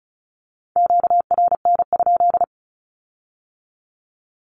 Morse code, heard as a keyed tone, about 36 wpm, 700 Hz. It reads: QRN?
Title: Q-Codes with Spoken Meaning - 35wpm